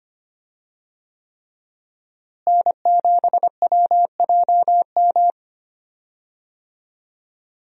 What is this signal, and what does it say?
Morse code, heard as a keyed tone, about 25 wpm, 700 Hz. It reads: N7WJM
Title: USA Call Signs - 5 Characters - Repeated 3x - 25wpm